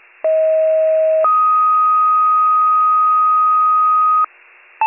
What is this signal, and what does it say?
Two-tone page, heard as a one-second tone, then a three-second tone, about 640 then 1250 Hz. It is heard from 0.2 to 4.2 s.